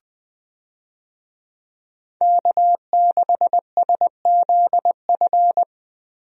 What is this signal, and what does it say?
Morse code, heard as a keyed tone, about 21 wpm, 700 Hz. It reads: K6SZF
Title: USA Call Signs - 5 Characters - Repeated 3x - 20wpm